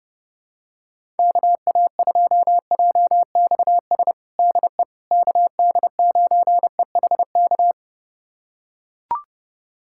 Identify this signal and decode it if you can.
Morse code, heard as a keyed tone, about 30 wpm, 700 Hz. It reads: KA2JXS DE KD9EHK
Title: USA Callsign DE Callsign - 6 Characters - 30wpm